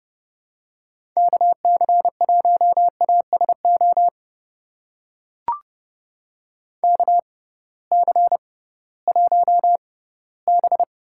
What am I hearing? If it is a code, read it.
Morse code, heard as a keyed tone, about 30 wpm, spaced as for 18 wpm, 700 Hz. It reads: KC1ASO KC1B
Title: USA Call Signs - 6 Characters - 6x Character Spacing - 30wpm